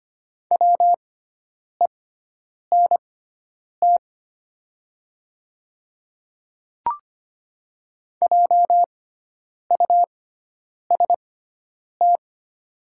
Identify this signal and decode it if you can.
Morse code, heard as a keyed tone, about 25 wpm, spaced as for 9 wpm, 700 Hz. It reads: WENT JUST
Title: 6x Character Spacing - Top 300 Words - 25wpm